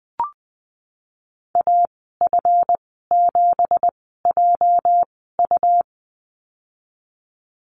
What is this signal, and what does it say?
Morse code, heard as a keyed tone, about 19 wpm, spaced as for 14 wpm, 700 Hz. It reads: AF7JU
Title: 213 Lesson 34g - Number 8 - Call Signs 20wpm